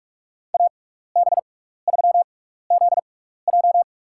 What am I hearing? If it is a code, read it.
Morse code, heard as a keyed tone, about 45 wpm, spaced as for 15 wpm, 700 Hz. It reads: AB3ZJ
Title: USA Call Signs - 5 Characters - 6x Character Spacing - 45wpm